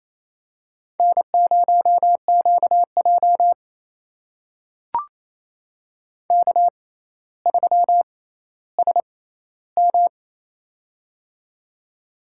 Morse code, heard as a keyed tone, about 28 wpm, spaced as for 14 wpm, 700 Hz. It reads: N0QJ K3SM
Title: USA Call Signs - 4 Characters - 6x Character Spacing - 28wpm